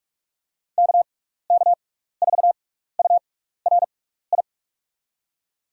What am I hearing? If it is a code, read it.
Morse code, heard as a keyed tone, about 42 wpm, spaced as for 15 wpm, 700 Hz. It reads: KK4URI
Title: USA Call Signs - 6x Character Spacing - 45wpm